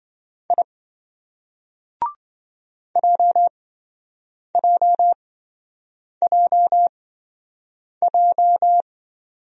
Morse code, heard as a keyed tone, about 27 wpm, spaced as for 7 wpm, 700 Hz. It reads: I JJJJ